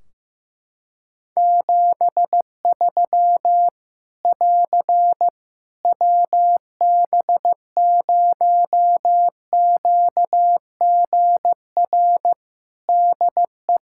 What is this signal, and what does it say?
Morse code, heard as a keyed tone, about 15 wpm, 700 Hz. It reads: 73 <AR> WB0QGR DE
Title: Practice QSO - Exchange 4 - Ending - 15wpm